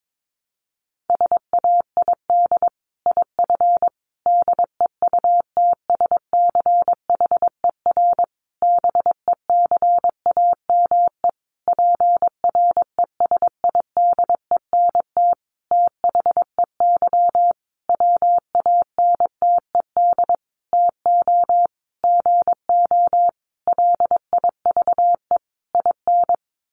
Morse code, heard as a keyed tone, about 22 wpm, 700 Hz. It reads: SAID IF DEUTSCHER BECAME PRESIDENT THEY WANTED TO GO LIVE IN
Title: A Sound Of Thunder 22wpm